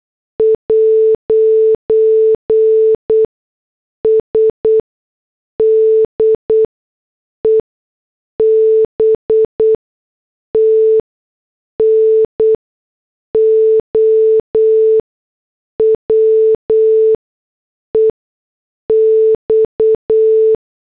Morse code, heard as a keyed tone, about 8 wpm, spaced as for 6 wpm, 440 Hz. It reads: 'S DEBT NOW EX
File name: morse / headlines